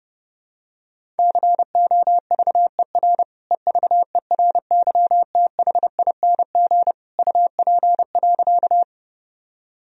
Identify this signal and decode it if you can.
Morse code, heard as a keyed tone, about 30 wpm, 700 Hz. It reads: COVER EVERYTHING UP.
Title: Sentences from Top 1000 Words - Review Part 1 - 30wpm